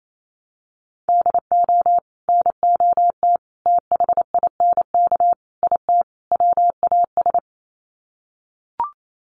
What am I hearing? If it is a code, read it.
Morse code, heard as a keyed tone, about 28 wpm, 700 Hz. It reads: DO NOT THINK IT WAS
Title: Sets of 5 Words Encore 28wpm